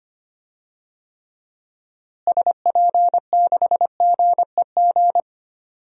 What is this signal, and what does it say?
Morse code, heard as a keyed tone, about 25 wpm, 700 Hz. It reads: SP6GEG
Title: DXCC-100 Call Signs - 6 Characters - Repeated 3x - 25wpm